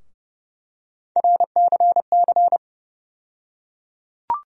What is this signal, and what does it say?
Morse code, heard as a keyed tone, about 31 wpm, 700 Hz. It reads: RCC
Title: USA Call Sign Postfixes - 3 Letters 30wpm